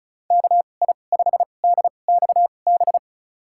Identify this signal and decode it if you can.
Morse code, heard as a keyed tone, about 34 wpm, spaced as for 25 wpm, 700 Hz. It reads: KI5DXB
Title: USA Call Signs - 2x Character Spacing - 35wpm